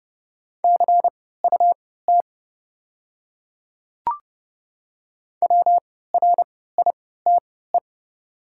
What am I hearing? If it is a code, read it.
Morse code, heard as a keyed tone, about 30 wpm, spaced as for 17 wpm, 700 Hz. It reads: CUT WRITE
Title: 3x Character Spacing - Top 300 Words - 30wpm